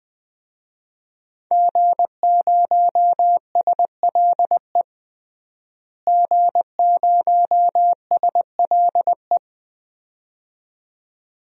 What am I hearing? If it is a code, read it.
Morse code, heard as a keyed tone, about 20 wpm, 700 Hz. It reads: G0SLE G0SLE
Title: DXCC-100 Call Signs - 5 Characters - Repeated 2x - 20wpm